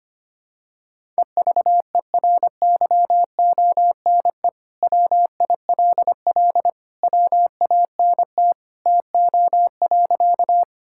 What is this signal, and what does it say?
Morse code, heard as a keyed tone, about 25 wpm, 700 Hz. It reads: EVERYONE WILL WANT TO.
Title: Sentences from Top 1000 Words - Review Part 5 - 25wpm